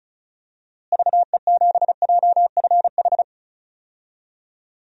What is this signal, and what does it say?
Morse code, heard as a keyed tone, about 34 wpm, 700 Hz. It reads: VE7JFH
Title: DXCC-100 Call Signs - 6 Characters - Repeated 2x - 35wpm